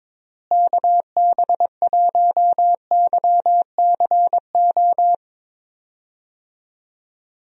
Morse code, heard as a keyed tone, about 22 wpm, 700 Hz. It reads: KB1YCO
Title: USA Call Signs - 6 Characters - Repeated 2x - 22wpm